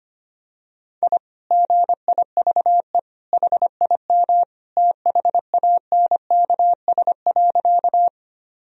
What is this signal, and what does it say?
Morse code, heard as a keyed tone, about 25 wpm, 700 Hz. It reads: I GIVE HIM THANKS.
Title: Sentences from Top 1000 Words - Review Part 3 - 25wpm